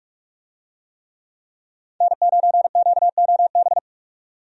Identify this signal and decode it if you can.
Morse code, heard as a keyed tone, about 46 wpm, 700 Hz. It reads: N9QOB
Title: USA Call Signs - 5 Characters - 45wpm